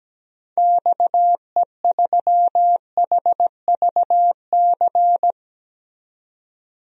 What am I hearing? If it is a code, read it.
Morse code, heard as a keyed tone, about 17 wpm, 700 Hz. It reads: XE3HVC
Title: DXCC-100 Call Signs - 6 Characters - 17wpm